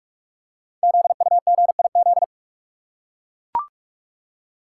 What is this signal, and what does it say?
Morse code, heard as a keyed tone, about 44 wpm, 700 Hz. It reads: ZUGIZ
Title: Code Groups - 5 Letter:Numbers - 45wpm